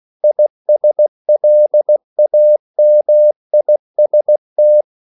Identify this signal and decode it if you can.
Morse code, heard as a keyed tone, about 16 wpm, 600 Hz. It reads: ISLAMIST